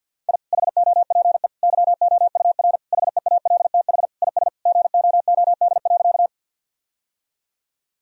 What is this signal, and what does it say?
Morse code, heard as a keyed tone, about 50 wpm, 700 Hz. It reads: I HOPE YOUR HEALTH IS GOOD.
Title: Sentences from Top 1000 Words - Review Part 3 - 50wpm